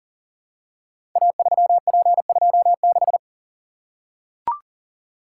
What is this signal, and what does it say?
Morse code, heard as a keyed tone, about 39 wpm, 690 Hz. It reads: A3P26
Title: Code Groups - 5 Letter:Numbers - 40wpm